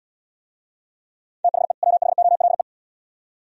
Morse code, heard as a keyed tone, about 78 wpm, 700 Hz. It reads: THE WHOLE